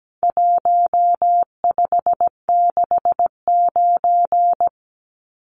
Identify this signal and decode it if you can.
Morse code, heard as a keyed tone, about 17 wpm, 700 Hz. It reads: 1569
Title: Numbers - Contest Style - 4-Digit 17wpm